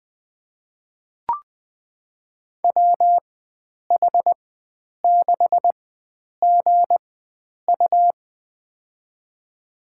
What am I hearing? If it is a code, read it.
Morse code, heard as a keyed tone, about 20 wpm, spaced as for 9 wpm, 700 Hz. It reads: WH6GU
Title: USA Call Signs - 5 Characters - 4x Character Spacing - 20wpm